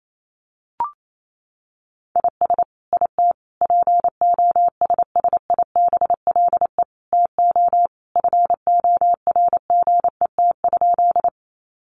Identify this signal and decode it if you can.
Morse code, heard as a keyed tone, about 28 wpm, 700 Hz. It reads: IS IT POSSIBLE TO FORGET?
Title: Sentences from Top 1200 Words 28wpm